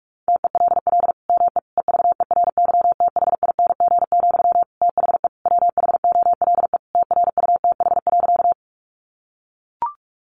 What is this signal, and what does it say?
Morse code, heard as a keyed tone, about 44 wpm, 700 Hz. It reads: TELL ME EVERYTHING, THE WHOLE TRUTH.
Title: Sentences from Top 800 Words 45wpm